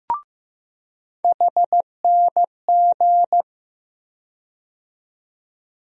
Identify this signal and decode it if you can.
Morse code, heard as a keyed tone, about 15 wpm, 700 Hz. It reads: HNG
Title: USA Call Sign Postfixes - 3 Letters 15wpm